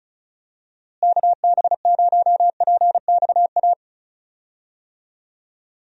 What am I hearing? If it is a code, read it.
Morse code, heard as a keyed tone, about 34 wpm, 700 Hz. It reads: KB0PXA
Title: USA Call Signs - 6 Characters - Repeated 3x - 35wpm